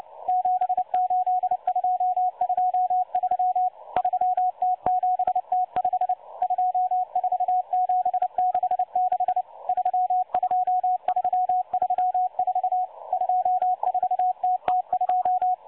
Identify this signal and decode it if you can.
Morse code, heard as a keyed tone, about 29 wpm, 730 Hz. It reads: 78223 3T7T5 24766 32334 24TT2
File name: M12-Morse-Numbers-Station-8116-kHz-CW